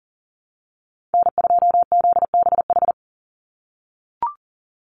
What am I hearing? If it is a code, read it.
Morse code, heard as a keyed tone, about 40 wpm, 700 Hz. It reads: N2ZBH
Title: USA Call Signs 40wpm